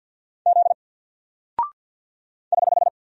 Morse code, heard as a keyed tone, about 49 wpm, 700 Hz. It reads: Z <HH>
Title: Single Character (Letter:Number:Punctuation:Prosign) 50wpm